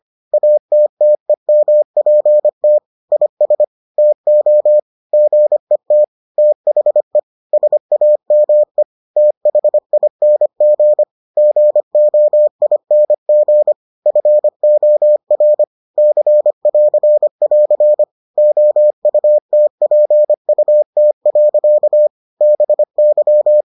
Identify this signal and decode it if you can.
Morse code, heard as a keyed tone, about 25 wpm, 600 Hz. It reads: ATTEMPT IS TO GET THE SAME THING GOING FOR C<AR><AR> OUTPUT. BY